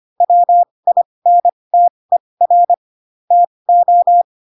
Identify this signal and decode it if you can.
Morse code, heard as a keyed tone, about 25 wpm, spaced as for 20 wpm, 700 Hz. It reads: WINTER TO
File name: Pinocchio / Pinocchio ch1 15 25 ik1hge